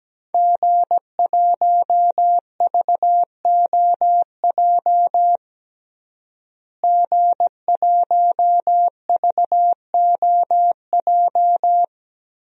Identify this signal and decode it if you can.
Morse code, heard as a keyed tone, about 17 wpm, 700 Hz. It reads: G1VOJ G1VOJ
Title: DXCC-100 Call Signs - 5 Characters - Repeated 3x - 17wpm